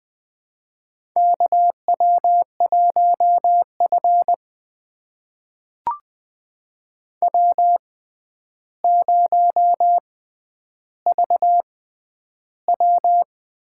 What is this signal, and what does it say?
Morse code, heard as a keyed tone, about 20 wpm, spaced as for 10 wpm, 700 Hz. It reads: KW1F W0VW
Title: USA Call Signs - 4 Characters - 6x Character Spacing - 20wpm